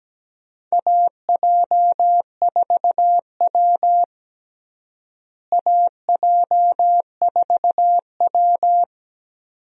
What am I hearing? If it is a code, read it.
Morse code, heard as a keyed tone, about 17 wpm, 700 Hz. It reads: AJ4W AJ4W
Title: USA Call Signs - 4 Characters - Repeated 2x - 17wpm